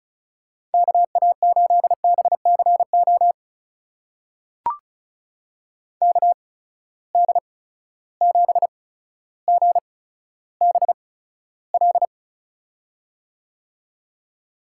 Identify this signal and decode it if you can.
Morse code, heard as a keyed tone, about 36 wpm, spaced as for 15 wpm, 700 Hz. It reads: KA8BCO KD7GBL
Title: USA Call Signs - 6 Characters - 8x Character Spacing - 35wpm